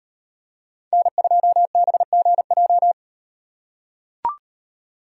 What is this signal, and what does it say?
Morse code, heard as a keyed tone, about 38 wpm, 700 Hz. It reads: N2BGJ